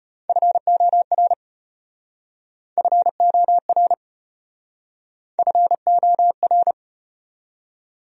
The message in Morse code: FOR FOR FOR

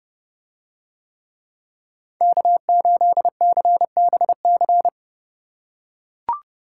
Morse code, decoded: K8CBC

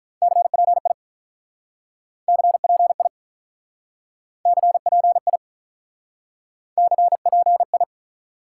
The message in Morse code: CPI CPI CPI CPI

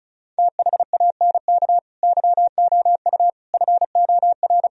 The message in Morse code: THANK YOU FOR

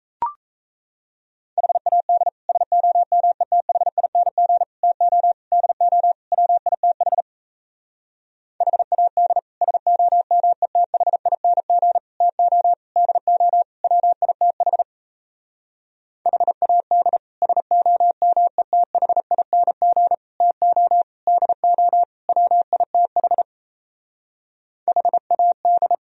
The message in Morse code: HAD SOMETHING TO DO WITH HAD SOMETHING TO DO WITH HAD SOMETHING TO DO WITH HAD

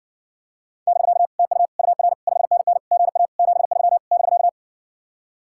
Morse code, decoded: <BK> TU UR 5NN GA 73 <BK>

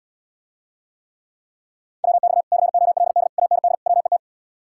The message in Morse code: WX COLD AND WET

